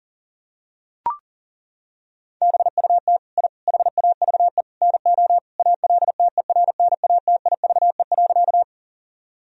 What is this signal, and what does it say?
Morse code, heard as a keyed tone, about 40 wpm, 700 Hz. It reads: BUT I HAVE NO ALTERNATIVE.
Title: Sentences from Top 1000 Words - Review Part 4 - 40wpm